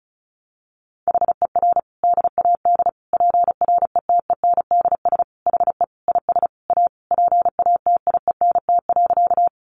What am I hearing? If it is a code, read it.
Morse code, heard as a keyed tone, about 34 wpm, 700 Hz. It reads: HER DAD PRETENDS HE IS A PATIENT.
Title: Additional Easy Phrases 35wpm